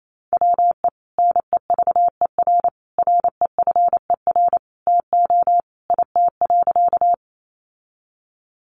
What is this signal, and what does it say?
Morse code, heard as a keyed tone, about 28 wpm, 700 Hz. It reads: WE NEVER REFER TO IT.
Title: Sentences from Top 1000 Words 28wpm